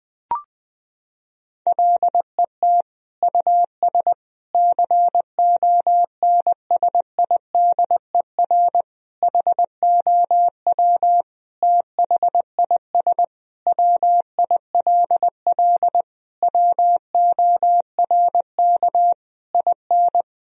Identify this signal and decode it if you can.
Morse code, heard as a keyed tone, about 21 wpm, 700 Hz. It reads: LET US CONSIDER HOW THIS WILL WORK IN
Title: Sentences from Top 1000 Words - Review Part 4 - 20wpm